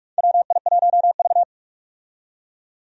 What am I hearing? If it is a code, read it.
Morse code, heard as a keyed tone, about 46 wpm, 700 Hz. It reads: WI1V
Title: USA Call Signs - 4 Characters - 45wpm